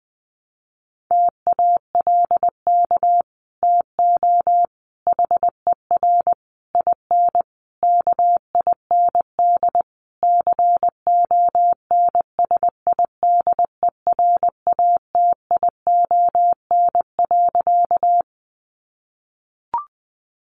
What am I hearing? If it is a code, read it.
Morse code, heard as a keyed tone, about 20 wpm, 700 Hz. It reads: TALK TO HER IN KIND CONSIDERATION.
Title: Sentences from Top 1500 Words 20wpm